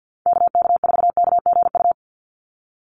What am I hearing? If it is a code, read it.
Morse code, heard as a keyed tone, about 50 wpm, 700 Hz. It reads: KK4KGU
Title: USA Call Signs 50wpm